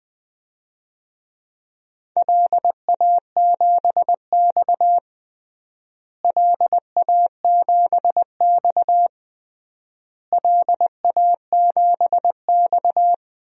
Morse code, heard as a keyed tone, about 20 wpm, 700 Hz. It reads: LA7X LA7X LA7X
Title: DXCC-100 Call Signs - 4 Characters - Repeated 3x - 20wpm